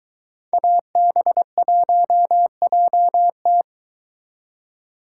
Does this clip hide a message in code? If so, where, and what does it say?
Yes: Morse code from 0.5 to 3.6 s, heard as a keyed tone, about 22 wpm, 700 Hz. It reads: AB1JT